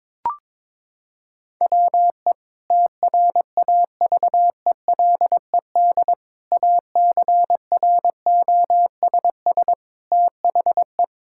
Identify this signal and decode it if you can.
Morse code, heard as a keyed tone, about 22 wpm, 700 Hz. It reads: WE TRAVELED ACROSS THE
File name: practice / 22wpm / Sentences with 2-Word Phrases - 22wpm